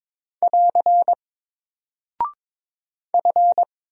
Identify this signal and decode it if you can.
Morse code, heard as a keyed tone, about 22 wpm, 700 Hz. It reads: <AR> F
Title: Single Character (Letter:Number:Punctuation:Prosign) Alt 22wpm